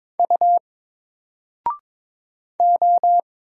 Morse code, heard as a keyed tone, about 22 wpm, 700 Hz. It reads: U O
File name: Single-Letter-Number-22wpm